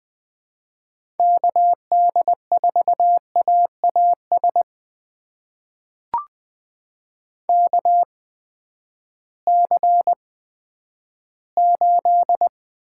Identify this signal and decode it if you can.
Morse code, heard as a keyed tone, about 20 wpm, spaced as for 11 wpm, 700 Hz. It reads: KD4AAS KC8
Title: USA Call Signs - 6 Characters - 8x Character Spacing - 20wpm